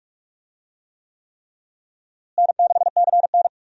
Morse code, heard as a keyed tone, about 46 wpm, 700 Hz. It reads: N6CN
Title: USA Call Signs - 4 Characters - Repeated 3x - 45wpm